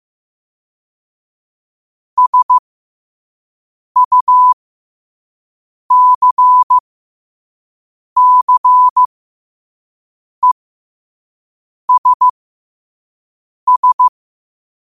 Morse code, heard as a keyed tone, about 15 wpm, spaced as for 5 wpm, 1000 Hz. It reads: SUCCESS
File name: qotdfiles / 5wpm